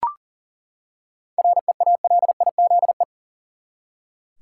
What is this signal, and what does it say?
Morse code, heard as a keyed tone, about 40 wpm, 700 Hz. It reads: REALIZE
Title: Seven Letter Words - 40wpm